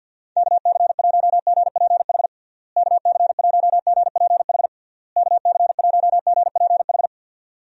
Morse code, heard as a keyed tone, about 50 wpm, 700 Hz. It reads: KC1GPH KC1GPH KC1GPH
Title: USA Call Signs - 6 Characters - Repeated 3x - 50wpm